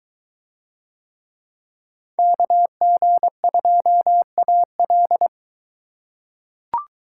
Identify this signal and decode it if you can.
Morse code, heard as a keyed tone, about 21 wpm, 700 Hz. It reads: KG2AL